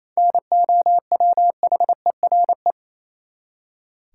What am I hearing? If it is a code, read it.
Morse code, heard as a keyed tone, about 28 wpm, 700 Hz. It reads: NOWHERE
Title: Seven Letter Words - 28wpm